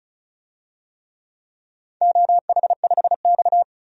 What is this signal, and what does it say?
Morse code, heard as a keyed tone, about 36 wpm, 700 Hz. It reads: OH5X
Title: DXCC-100 Call Signs - 4 Characters - 35wpm